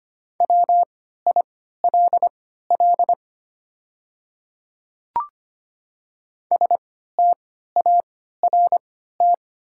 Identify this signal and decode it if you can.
Morse code, heard as a keyed tone, about 25 wpm, spaced as for 14 wpm, 700 Hz. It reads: WILL START
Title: 3x Character Spacing - Top 300 Words - 25wpm